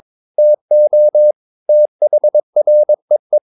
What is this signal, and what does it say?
Morse code, heard as a keyed tone, about 22 wpm, 600 Hz. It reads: TO THREE
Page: Romanen Never af Ken Follet på engelsk.